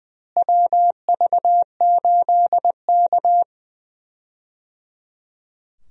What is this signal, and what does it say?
Morse code, heard as a keyed tone, about 20 wpm, 700 Hz. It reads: WV8K
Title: USA Call Signs - 4 Characters - Repeated 3x - 20wpm